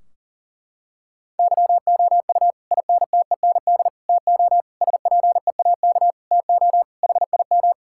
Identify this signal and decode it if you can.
Morse code, heard as a keyed tone, about 40 wpm, 700 Hz. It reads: YOU INTEND TO SPEAK TO HIM
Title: Sentences from Top 1600 Words 40wpm